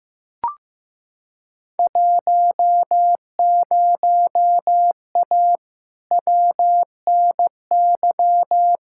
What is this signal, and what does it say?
Morse code, heard as a keyed tone, about 15 wpm, 700 Hz. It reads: 10A WNY
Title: Field Day Exchanges - 15wpm